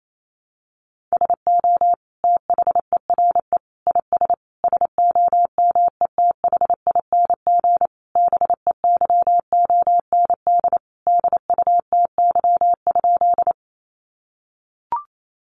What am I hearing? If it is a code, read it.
Morse code, heard as a keyed tone, about 28 wpm, 700 Hz. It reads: SO THERE IS SOMETHING BEYOND DUTY?
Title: Sentences from Top 1500 Words 28wpm